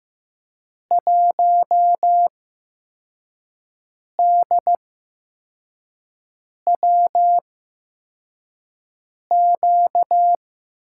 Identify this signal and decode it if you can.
Morse code, heard as a keyed tone, about 15 wpm, spaced as for 4 wpm, 700 Hz. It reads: 1DWQ
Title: USA Call Signs - 6 Characters - 8x Character Spacing - 15wpm